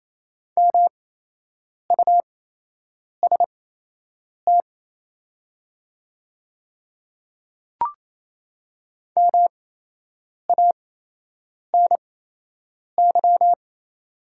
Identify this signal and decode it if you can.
Morse code, heard as a keyed tone, about 27 wpm, spaced as for 8 wpm, 690 Hz. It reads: MUST MANY
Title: 8x Character Spacing - Top 300 Words - 28wpm